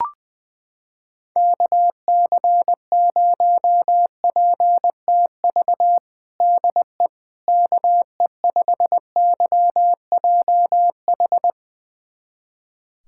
Morse code, heard as a keyed tone, about 21 wpm, 700 Hz. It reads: KC0PTV DE KE5YJH
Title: USA Callsign DE Callsign 20wpm